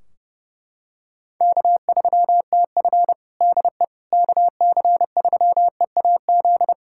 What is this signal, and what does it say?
Morse code, heard as a keyed tone, about 30 wpm, 700 Hz. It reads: K3TF DE KC3EAZ
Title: Practice QSO - Exchange 4 - Ending - 30wpm